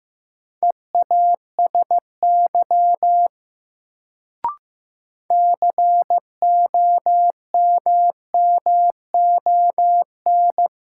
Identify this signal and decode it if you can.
Morse code, heard as a keyed tone, about 15 wpm, 700 Hz. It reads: EASY COMMON
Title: Sentences from Top 700 Words - New Word Drill - 15wpm